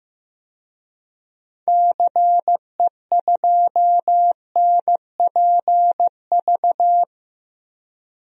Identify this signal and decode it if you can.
Morse code, heard as a keyed tone, about 15 wpm, 700 Hz. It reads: CE2NPV
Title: DXCC-100 Call Signs - 6 Characters - 15wpm